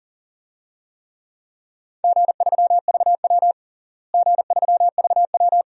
Morse code, heard as a keyed tone, about 37 wpm, 700 Hz. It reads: G3VW G3VW
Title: DXCC-100 Call Signs - 4 Characters - Repeated 2x - 40wpm